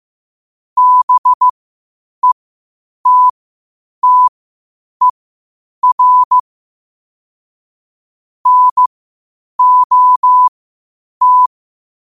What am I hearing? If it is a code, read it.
Morse code, heard as a keyed tone, about 15 wpm, spaced as for 8 wpm, 1000 Hz. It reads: BETTER NOT
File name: qotdfiles / 8wpm